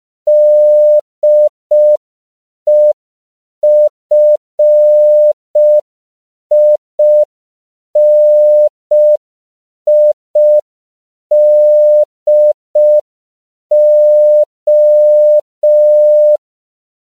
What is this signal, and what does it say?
Morse code, heard as a keyed tone, about 5 wpm, 600 Hz. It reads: DEFINIDO
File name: CW / 5PPM